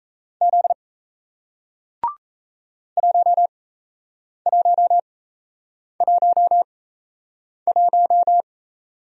Morse code, heard as a keyed tone, about 38 wpm, spaced as for 8 wpm, 700 Hz. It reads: Z 1111